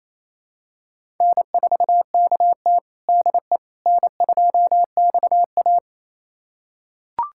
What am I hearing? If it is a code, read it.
Morse code, heard as a keyed tone, about 28 wpm, 700 Hz. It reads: N4KT DE N2XA
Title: USA Callsign DE Callsign - 4 Characters - 28wpm